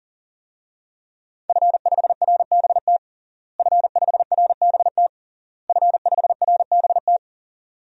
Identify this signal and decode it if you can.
Morse code, heard as a keyed tone, about 40 wpm, 700 Hz. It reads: F5RBT F5RBT F5RBT
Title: DXCC-100 Call Signs - 5 Characters - Repeated 3x - 40wpm